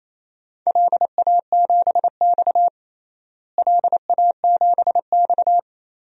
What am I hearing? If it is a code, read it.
Morse code, heard as a keyed tone, about 27 wpm, 700 Hz. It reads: LA7X LA7X
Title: DXCC-100 Call Signs - 4 Characters - Repeated 3x - 28wpm